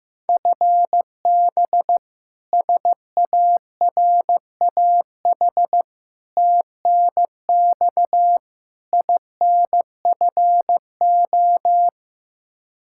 Morse code, heard as a keyed tone, about 15 wpm, 700 Hz. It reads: FB SARAH TNX INFO
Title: K1USN SST QSO Practice 15wpm